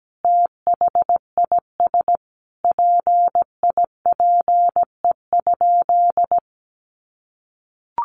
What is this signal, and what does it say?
Morse code, heard as a keyed tone, about 17 wpm, 700 Hz. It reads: THIS PIPE?
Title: Sentences from Top 1500 Words 17wpm